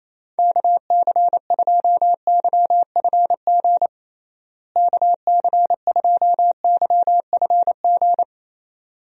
Morse code, heard as a keyed tone, about 28 wpm, 700 Hz. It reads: KC2YFG KC2YFG
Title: USA Call Signs - 6 Characters - Repeated 2x - 28wpm